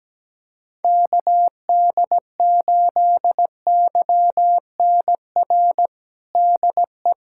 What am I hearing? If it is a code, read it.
Morse code, heard as a keyed tone, about 17 wpm, 700 Hz. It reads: KD8YNR DE
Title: USA Callsign DE Callsign 17wpm